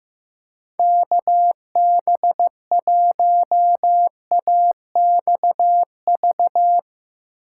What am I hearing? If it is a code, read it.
Morse code, heard as a keyed tone, about 16 wpm, 700 Hz. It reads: KB1AXV
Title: Speed Racing - USA 2x3 Call Signs 15wpm